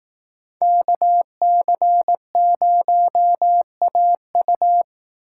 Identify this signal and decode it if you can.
Morse code, heard as a keyed tone, about 18 wpm, 700 Hz. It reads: KC0AU
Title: Speed Racing - USA 2x2 Call Signs 15wpm